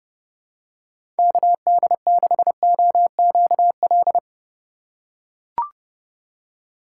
Morse code, heard as a keyed tone, about 32 wpm, 700 Hz. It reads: KD6OQL
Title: USA Call Signs - 6 Characters - 4x Character Spacing - 30wpm